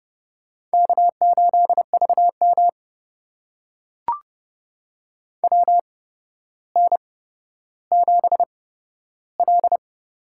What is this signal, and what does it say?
Morse code, heard as a keyed tone, about 30 wpm, spaced as for 13 wpm, 700 Hz. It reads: K8VM WN7L
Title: USA Call Signs - 4 Characters - 8x Character Spacing - 30wpm